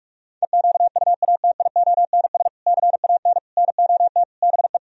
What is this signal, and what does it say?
Morse code, heard as a keyed tone, about 49 wpm, 700 Hz. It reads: EQUATIONS CAN NOT BE